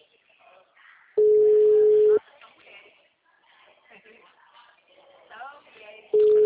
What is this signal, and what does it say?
Ringback tone (Europe, 425 Hz), with speech.
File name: out- -2005- - - .39257